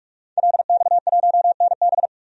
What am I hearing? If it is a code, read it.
Morse code, heard as a keyed tone, about 46 wpm, 700 Hz. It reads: LX1NB
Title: DXCC-100 Call Signs - 5 Characters - 45wpm